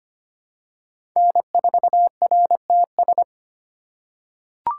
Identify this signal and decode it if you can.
Morse code, heard as a keyed tone, about 25 wpm, 700 Hz. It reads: N4RTS
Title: USA Call Signs - 4x Character Spacing - 25wpm